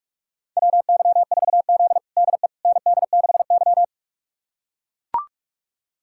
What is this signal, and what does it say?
Morse code, heard as a keyed tone, about 42 wpm, 700 Hz. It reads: WY4Z DE ND6Y
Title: USA Callsign DE Callsign - 4 Characters - 45wpm